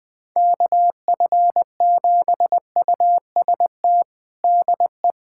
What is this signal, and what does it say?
Morse code, heard as a keyed tone, about 21 wpm, 700 Hz. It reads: KF7UST DE
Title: USA Callsign DE Callsign 20wpm